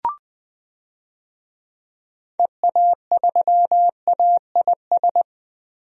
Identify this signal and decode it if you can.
Morse code, heard as a keyed tone, about 20 wpm, 700 Hz. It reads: EA3AIS
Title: DXCC-100 Call Signs - 6 Characters - Repeated 2x - 20wpm